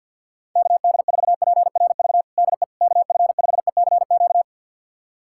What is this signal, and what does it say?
Morse code, heard as a keyed tone, about 52 wpm, 700 Hz. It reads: KD4PRV DE KF5ECQ
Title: USA Callsign DE Callsign - 6 Characters - 50wpm